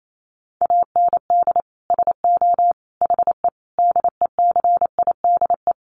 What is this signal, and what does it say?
Morse code, heard as a keyed tone, about 28 wpm, 700 Hz. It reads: AND SO HE DECIDE
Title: Sentences from Top 500 Words 28wpm